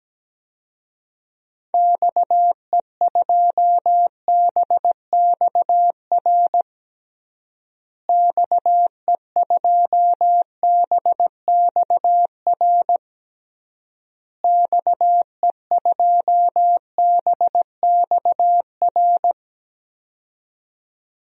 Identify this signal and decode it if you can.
Morse code, heard as a keyed tone, about 17 wpm, 700 Hz. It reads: XE2BXR XE2BXR XE2BXR
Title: DXCC-100 Call Signs - 6 Characters - Repeated 3x - 17wpm